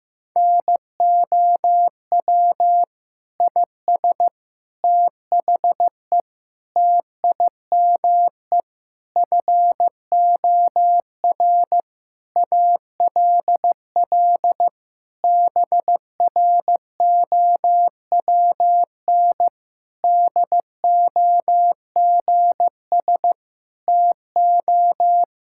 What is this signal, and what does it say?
Morse code, heard as a keyed tone, about 15 wpm, 700 Hz. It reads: NOW IS THE TIME FOR ALL BROWN DOGS TO
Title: Pangram Challenge v2 - 15wpm